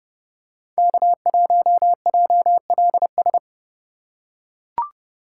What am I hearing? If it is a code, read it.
Morse code, heard as a keyed tone, about 30 wpm, 700 Hz. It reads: K1JLS
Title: USA Call Signs - 5 Characters - 2x Character Spacing - 30wpm